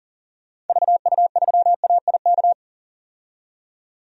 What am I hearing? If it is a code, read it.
Morse code, heard as a keyed tone, about 38 wpm, 700 Hz. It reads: VU3AIK
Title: DXCC-100 Call Signs - 6 Characters - 40wpm